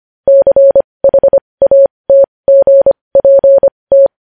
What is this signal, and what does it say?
Morse code, heard as a keyed tone, about 25 wpm, spaced as for 20 wpm, 550 Hz. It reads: CHATGPT
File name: podcast / News headlines-1x-25x20